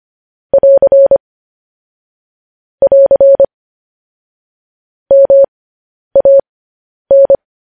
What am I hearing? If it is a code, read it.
Morse code, heard as a keyed tone, about 25 wpm, spaced as for 10 wpm, 550 Hz. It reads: <AR> <AR> MAN
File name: podcast / News headlines-1x-25x10